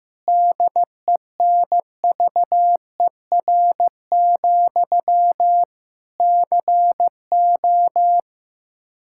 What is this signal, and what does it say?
Morse code, heard as a keyed tone, about 15 wpm, 700 Hz. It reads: DENVER, CO
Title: K1USN SST QSO Practice 15wpm